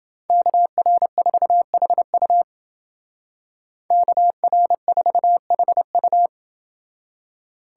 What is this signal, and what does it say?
Morse code, heard as a keyed tone, about 29 wpm, 700 Hz. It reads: KR4HU KR4HU